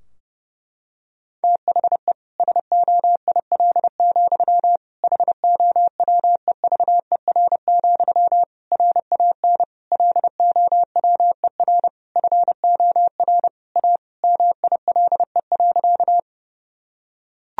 Morse code, heard as a keyed tone, about 32 wpm, 700 Hz. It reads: THE SOIL, HOWEVER, RAN LOWER FOR A MILE.
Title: Sentences from Top 1000 Words - Review Part 4 - 30wpm